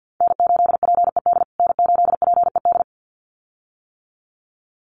Morse code, heard as a keyed tone, about 50 wpm, 700 Hz. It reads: N8PED N8PED
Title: USA Call Signs Repeated 2x 50wpm